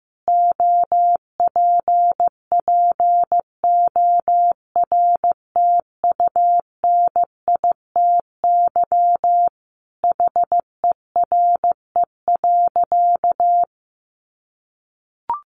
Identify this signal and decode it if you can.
Morse code, heard as a keyed tone, about 15 wpm, 700 Hz. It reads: OPPORTUNITY HERE.
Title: Sentences from Top 1300 Words 15wpm